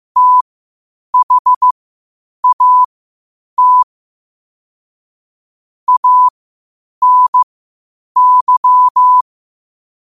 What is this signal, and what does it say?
Morse code, heard as a keyed tone, about 15 wpm, spaced as for 8 wpm, 1000 Hz. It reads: THAT ANY
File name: qotdfiles / 8wpm